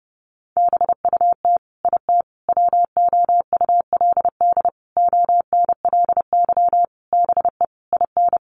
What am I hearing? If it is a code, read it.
Morse code, heard as a keyed tone, about 30 wpm, 700 Hz. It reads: BUT IT WOULD ONLY BE IN
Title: Sentences from Top 1500 Words 30wpm